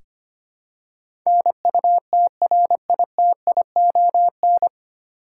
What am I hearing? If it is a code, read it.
Morse code, heard as a keyed tone, about 25 wpm, 700 Hz. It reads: NUTRITION
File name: Nine Letter Words - 25wpm